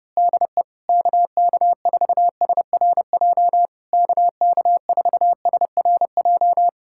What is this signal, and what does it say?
Morse code, heard as a keyed tone, about 29 wpm, 700 Hz. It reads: DE KK4SRJ KK4SRJ
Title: Speed-Racing Common QSO Phrases - 25wpm